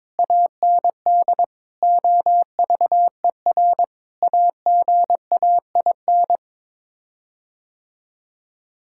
Morse code, heard as a keyed tone, about 22 wpm, 700 Hz. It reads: AND OVER AGAIN
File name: practice / 22wpm / Sets of 7 Words - 22wpm